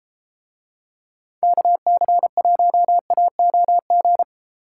Morse code, heard as a keyed tone, about 33 wpm, 700 Hz. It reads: KC1AOG